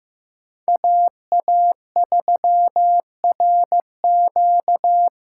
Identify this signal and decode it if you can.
Morse code, heard as a keyed tone, about 15 wpm, 700 Hz. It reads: AA3RQ
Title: USA Call Signs - 6x Character Spacing - 15wpm